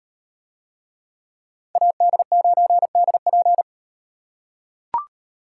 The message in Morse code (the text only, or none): AD9DP